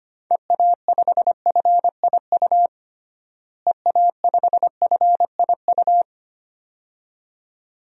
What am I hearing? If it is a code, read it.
Morse code, heard as a keyed tone, about 25 wpm, 700 Hz. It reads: EA5FIU EA5FIU
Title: DXCC-100 Call Signs - 6 Characters - Repeated 2x - 25wpm